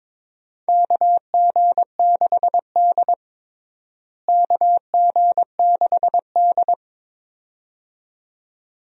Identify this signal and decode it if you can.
Morse code, heard as a keyed tone, about 22 wpm, 700 Hz. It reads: KG6D KG6D
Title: USA Call Signs - 4 Characters - Repeated 2x - 22wpm